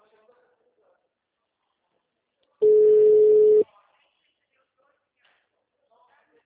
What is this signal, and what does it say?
Ringback tone (Europe, 425 Hz), with speech.
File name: out- -2029- - - .42574